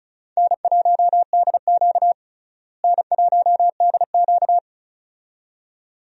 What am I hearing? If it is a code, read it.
Morse code, heard as a keyed tone, about 36 wpm, 700 Hz. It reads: N1DQ N1DQ
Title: USA Call Signs - 4 Characters - Repeated 2x - 35wpm